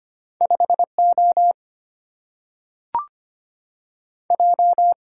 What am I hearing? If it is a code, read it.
Morse code, heard as a keyed tone, about 25 wpm, 700 Hz. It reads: 5 O J
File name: Two Letter-Number Combinations - 25wpm